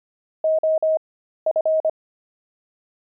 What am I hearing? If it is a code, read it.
Morse code, heard as a keyed tone, about 25 wpm, spaced as for 13 wpm, 630 Hz. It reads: OF
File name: pangram-fs-13-cs-25-gloved